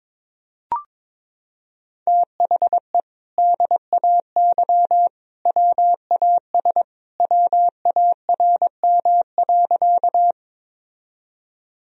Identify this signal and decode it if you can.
Morse code, heard as a keyed tone, about 22 wpm, 700 Hz. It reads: THE DAY WAS WARM.
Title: Sentences from Top 1000 Words - Review Part 2 - 22wpm